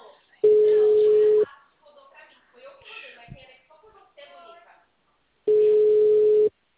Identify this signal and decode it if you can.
Ringback tone (Europe, 425 Hz), with speech.
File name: out- -2004- - - .8931